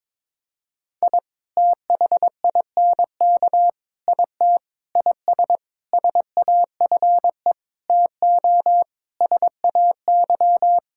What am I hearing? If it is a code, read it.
Morse code, heard as a keyed tone, about 22 wpm, 700 Hz. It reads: I THINK IT IS SAFE TO SAY
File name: practice / 22wpm / Sentences with 8-Word Phrases - 22wpm